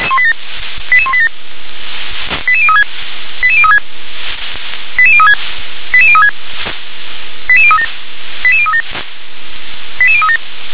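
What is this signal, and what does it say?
Five-tone selective call (ZVEI), heard as quick runs of five tones, tones of about 70 ms. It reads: 88187 88036 88036 88036 88036 88037 88037 88037
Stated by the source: ZVEI five-tone selective call 88182, received on an SDR.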